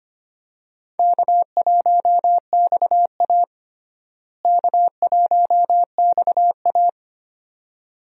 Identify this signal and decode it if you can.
Morse code, heard as a keyed tone, about 25 wpm, 700 Hz. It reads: K1XA K1XA
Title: USA Call Signs - 4 Characters - Repeated 2x - 25wpm